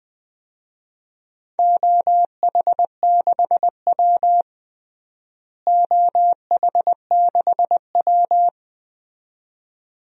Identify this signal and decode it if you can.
Morse code, heard as a keyed tone, about 20 wpm, 700 Hz. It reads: OH6W OH6W
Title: DXCC-100 Call Signs - 4 Characters - Repeated 2x - 20wpm